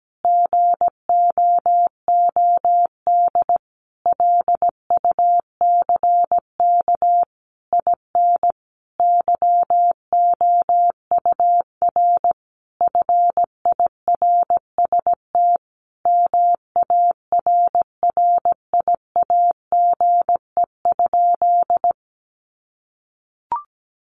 Morse code, heard as a keyed tone, about 17 wpm, 700 Hz. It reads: GOOD LUCK IN YOUR FIRST MARRIAGE?
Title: Sentences from Top 1500 Words 17wpm